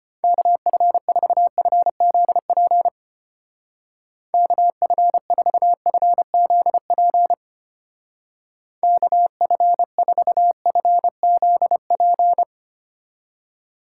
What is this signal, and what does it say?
Morse code, heard as a keyed tone, about 32 wpm, 700 Hz. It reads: KF4FZP KF4FZP KF4FZP